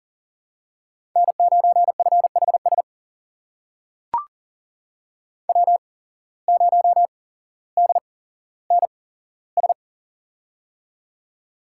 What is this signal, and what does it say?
Morse code, heard as a keyed tone, about 40 wpm, spaced as for 17 wpm, 700 Hz. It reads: N9FHS W0DNS
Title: USA Call Signs - 5 Characters - 8x Character Spacing - 40wpm